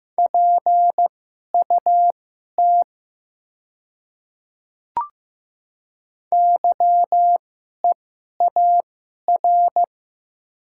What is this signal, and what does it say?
Morse code, heard as a keyed tone, about 15 wpm, spaced as for 11 wpm, 700 Hz. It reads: PUT YEAR
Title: 2x Character Spacing - Top 200 Words - 15wpm